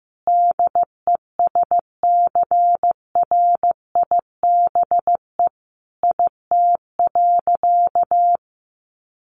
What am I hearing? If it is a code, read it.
Morse code, heard as a keyed tone, about 15 wpm, 700 Hz. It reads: DESCRIBE IT.
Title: Sentences from Top 1100 Words 15wpm